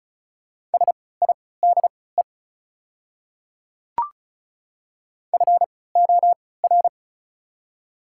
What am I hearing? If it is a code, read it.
Morse code, heard as a keyed tone, about 34 wpm, spaced as for 20 wpm, 700 Hz. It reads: SIDE FOR
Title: 3x Character Spacing - Top 200 Words - 35wpm